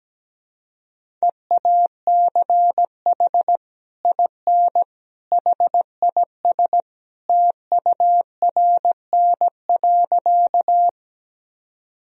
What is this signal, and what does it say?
Morse code, heard as a keyed tone, about 17 wpm, 700 Hz. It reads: EACH IN HIS TURN.
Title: Sentences from Top 1000 Words - Review Part 1 - 17wpm